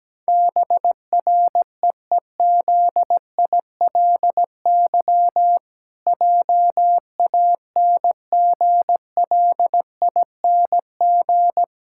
Morse code, heard as a keyed tone, about 17 wpm, 700 Hz. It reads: BREEZILY JANGLING
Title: Pangram Challenge v2 - 17wpm